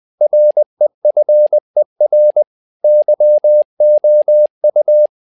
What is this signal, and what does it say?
Morse code, heard as a keyed tone, about 20 wpm, 600 Hz. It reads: REFER YOU
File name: tower trouble 20wpm